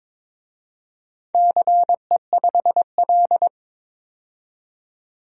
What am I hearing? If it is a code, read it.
Morse code, heard as a keyed tone, about 22 wpm, 700 Hz. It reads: CE5L
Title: DXCC-100 Call Signs - 4 Characters - 22wpm